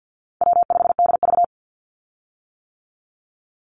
Morse code, heard as a keyed tone, about 50 wpm, 700 Hz. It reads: W5DV
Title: Easy Call Signs - 50wpm